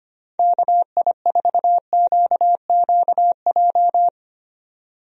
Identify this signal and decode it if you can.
Morse code, heard as a keyed tone, about 25 wpm, 700 Hz. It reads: KI4QQJ
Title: USA Call Signs - 6 Characters - Repeated 3x - 25wpm